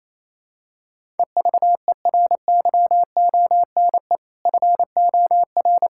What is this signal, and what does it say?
Morse code, heard as a keyed tone, about 28 wpm, 700 Hz. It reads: EVERYONE FOR
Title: Sentences from Top 1600 Words 28wpm